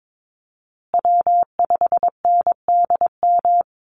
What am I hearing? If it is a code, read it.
Morse code, heard as a keyed tone, about 22 wpm, 700 Hz. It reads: W5NDM
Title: USA Call Signs Repeated 2x 22wpm